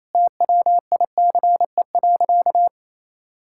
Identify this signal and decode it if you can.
Morse code, heard as a keyed tone, about 28 wpm, 700 Hz. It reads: TWICE.
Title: Sentences from Top 1600 Words 28wpm